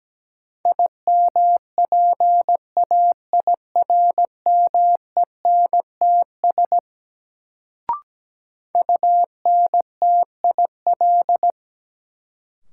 Morse code, heard as a keyed tone, about 17 wpm, 700 Hz. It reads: IMPAIRMENTS UNTIL
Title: Sets of 5 Words Encore - New Word Drill - 17wpm